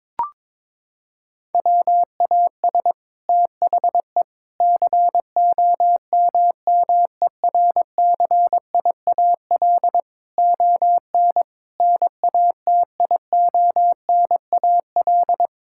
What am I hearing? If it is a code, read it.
Morse code, heard as a keyed tone, about 22 wpm, 700 Hz. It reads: WAS THE COMMERCIAL ON NATIONAL
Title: Sentences from Top 1000 Words - Review Part 5 - 22wpm